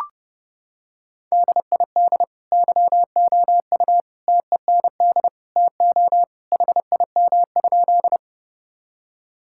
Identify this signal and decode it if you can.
Morse code, heard as a keyed tone, about 32 wpm, 700 Hz. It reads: DID YOU TEND TO HIM?
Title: Sentences from Top 1000 Words - Review Part 5 - 30wpm